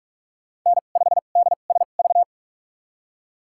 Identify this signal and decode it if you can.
Morse code, heard as a keyed tone, about 46 wpm, spaced as for 32 wpm, 700 Hz. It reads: N5DSV
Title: USA Call Signs - 5 Characters - 2x Character Spacing - 45wpm